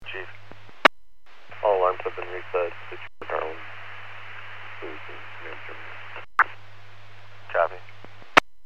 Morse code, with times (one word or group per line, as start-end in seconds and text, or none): none